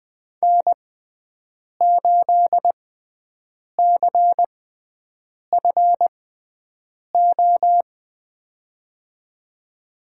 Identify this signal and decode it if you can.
Morse code, heard as a keyed tone, about 21 wpm, spaced as for 7 wpm, 700 Hz. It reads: N8CFO
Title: USA Call Signs - 5 Characters - 6x Character Spacing - 20wpm